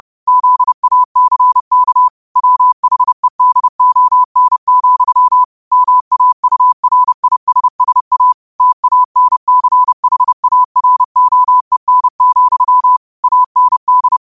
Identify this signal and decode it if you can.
Morse code, heard as a keyed tone, about 30 wpm, 1000 Hz. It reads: ZACK WHEDON, MAURISSA TANCHAROEN, AND
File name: qotdfiles / 30wpm